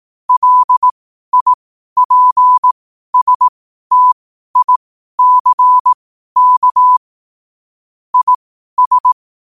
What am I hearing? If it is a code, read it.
Morse code, heard as a keyed tone, about 18 wpm, spaced as for 12 wpm, 1000 Hz. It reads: LIPSTICK IS
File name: qotdfiles / 12wpm